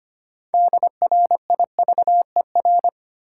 Morse code, heard as a keyed tone, about 25 wpm, 700 Hz. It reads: DRIVER
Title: Sentences from Top 1000 Words - New Word Drill - 25wpm